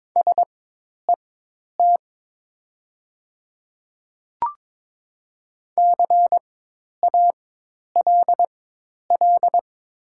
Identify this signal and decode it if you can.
Morse code, heard as a keyed tone, about 21 wpm, spaced as for 10 wpm, 700 Hz. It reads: SET CALL
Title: 4x Character Spacing - Top 200 Words - 22wpm